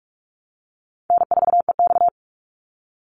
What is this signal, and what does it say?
Morse code, heard as a keyed tone, about 44 wpm, 700 Hz. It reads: N4EX
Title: NA SOTA Activator Callsigns 45wpm